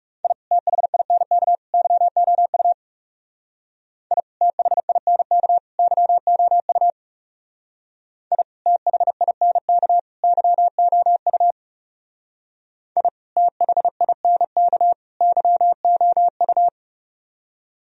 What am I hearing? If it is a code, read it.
Morse code, heard as a keyed tone, about 36 wpm, 700 Hz. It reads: I THINK YOU I THINK YOU I THINK YOU I THINK YOU
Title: Speed-Racing - Sets of 3 Words 30wpm